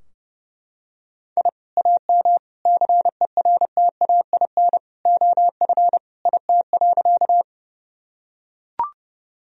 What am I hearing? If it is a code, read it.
Morse code, heard as a keyed tone, about 29 wpm, 700 Hz. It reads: I AM CERTAIN OF IT.
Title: Sentences from Top 1000 Words - Review Part 2 - 30wpm